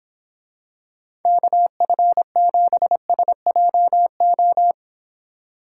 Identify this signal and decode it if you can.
Morse code, heard as a keyed tone, about 26 wpm, 700 Hz. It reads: KF7SJO